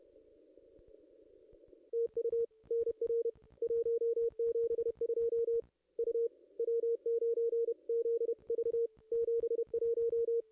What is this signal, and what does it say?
Morse code, heard as a keyed tone, about 30 wpm, 440 Hz. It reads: TU NR 172 U W9ZV 71
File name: contests / ss-cw-k5tr-op-n5rz -1600